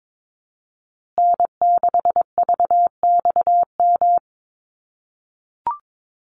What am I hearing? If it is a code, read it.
Morse code, heard as a keyed tone, about 22 wpm, 700 Hz. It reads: N6VXM
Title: USA Call Signs Repeated 3x 22wpm